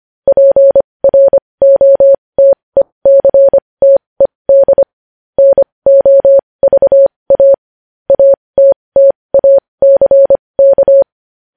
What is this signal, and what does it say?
Morse code, heard as a keyed tone, about 25 wpm, spaced as for 20 wpm, 550 Hz. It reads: PROTECTED NOVA ATTACK
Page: news-headlines-in-morse-code-at-20-wpm